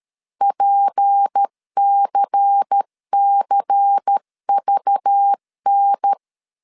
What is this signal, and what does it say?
Morse code, heard as a keyed tone, about 13 wpm, 800 Hz. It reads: PCCVN
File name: CW / DET / 1 6 1500C S C 12wpm 3533